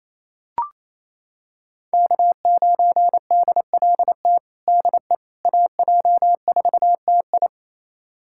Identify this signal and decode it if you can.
Morse code, heard as a keyed tone, about 28 wpm, 700 Hz. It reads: K9DLT DE AJ4TI
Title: USA Callsign DE Callsign - 5 Characters - 28wpm